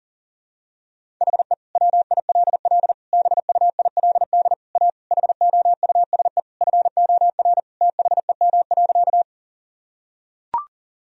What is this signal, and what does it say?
Morse code, heard as a keyed tone, about 40 wpm, 700 Hz. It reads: HE WILL BUILD A HOUSE FOR THEM.
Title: Sentences from Top 1000 Words - Review Part 1 - 40wpm